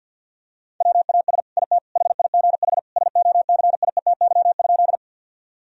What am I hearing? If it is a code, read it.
Morse code, heard as a keyed tone, about 48 wpm, 700 Hz. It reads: WAS IT HIGH SOCIETY?
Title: Sentences from Top 1000 Words - Review Part 4 - 50wpm